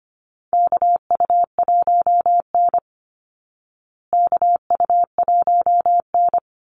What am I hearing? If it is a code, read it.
Morse code, heard as a keyed tone, about 25 wpm, 700 Hz. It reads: KU1N KU1N
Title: Call Signs Repeated 2x - 25wpm